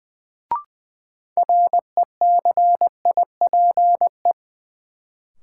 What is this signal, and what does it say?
Morse code, heard as a keyed tone, about 20 wpm, 700 Hz. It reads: RECIPE
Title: Sentences from Top 1200 Words - New Word Drill - 20wpm